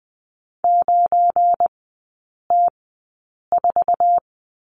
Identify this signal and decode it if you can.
Morse code, heard as a keyed tone, about 20 wpm, spaced as for 8 wpm, 700 Hz. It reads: 9T4
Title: 172 Lesson 28e - Number 9 - Three Characters 20wpm